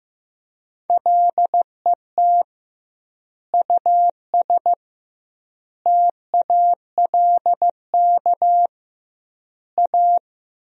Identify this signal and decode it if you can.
Morse code, heard as a keyed tone, about 16 wpm, 700 Hz. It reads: LET US TALK A
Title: 2x Word-Spacing - Sentences from Top 400 Words - 15wpm